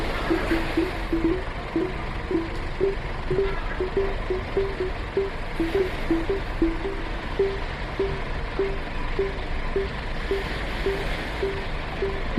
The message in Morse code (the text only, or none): none